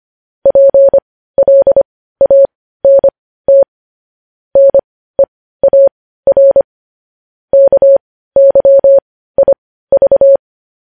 Morse code, heard as a keyed tone, about 25 wpm, spaced as for 15 wpm, 550 Hz. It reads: PLANT NEAR KYIV
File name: podcast / News headlines-1x-25x15